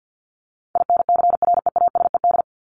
Morse code, heard as a keyed tone, about 50 wpm, 700 Hz. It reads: INCREASED
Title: ED Suffix - Top 100 Words with Base Words 50wpm